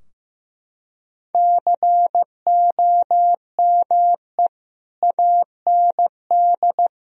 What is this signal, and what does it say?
Morse code, heard as a keyed tone, about 15 wpm, 700 Hz. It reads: COME AND
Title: Sentences from Top 1000 Words - Review Part 4 - 15wpm